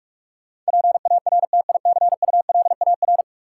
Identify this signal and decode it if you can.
Morse code, heard as a keyed tone, about 46 wpm, 700 Hz. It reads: PARTICULAR
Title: Sentences from Top 600 Words - New Word Drill - 45wpm